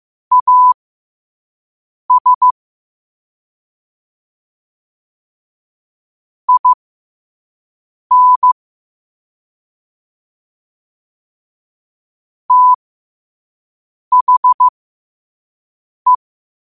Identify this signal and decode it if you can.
Morse code, heard as a keyed tone, about 15 wpm, spaced as for 5 wpm, 1000 Hz. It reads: AS IN THE